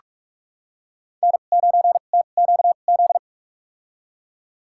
Morse code, heard as a keyed tone, about 46 wpm, spaced as for 33 wpm, 700 Hz. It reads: N9TQZ
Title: USA Call Signs - 5 Characters - 2x Character Spacing - 45wpm